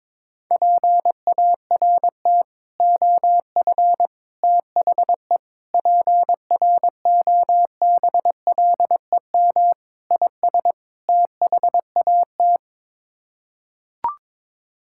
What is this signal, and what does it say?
Morse code, heard as a keyed tone, about 22 wpm, 700 Hz. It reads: PART OF THE PROBLEM IS THAT
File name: practice / 22wpm / Sets of 6 Words - 22wpm